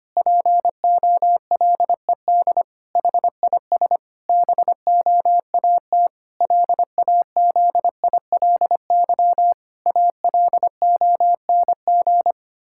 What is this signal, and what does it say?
Morse code, heard as a keyed tone, about 25 wpm, 700 Hz. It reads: POLED HIS BOAT LAZILY ALONG
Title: Pangram Challenge v2 - 25wpm